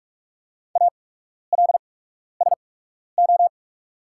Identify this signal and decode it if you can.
Morse code, heard as a keyed tone, about 47 wpm, spaced as for 12 wpm, 690 Hz. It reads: ALSO
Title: 8x Character Spacing - Top 200 Words - 45wpm